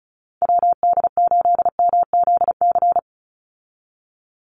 Morse code, heard as a keyed tone, about 34 wpm, 700 Hz. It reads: WD8MZC
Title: USA Call Signs 35wpm